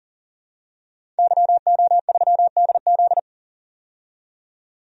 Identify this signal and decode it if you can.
Morse code, heard as a keyed tone, about 40 wpm, 700 Hz. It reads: YO3DZ
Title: DXCC-100 Call Signs - 5 Characters - 40wpm